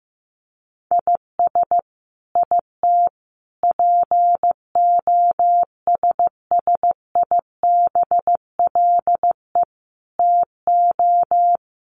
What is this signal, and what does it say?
Morse code, heard as a keyed tone, about 15 wpm, 700 Hz. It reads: IS IT POSSIBLE TO
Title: Sentences from Top 1200 Words 15wpm